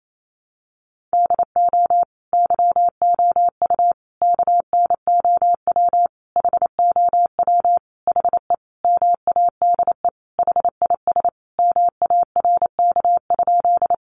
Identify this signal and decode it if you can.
Morse code, heard as a keyed tone, about 28 wpm, 700 Hz. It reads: DO YOU KNOW HOW HE MADE HIS MARK?
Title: Sentences from Top 300 Words 28wpm